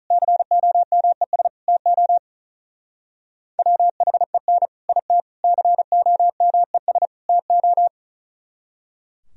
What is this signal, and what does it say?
Morse code, heard as a keyed tone, about 44 wpm, 700 Hz. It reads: COMES TO WHEN IT COMES TO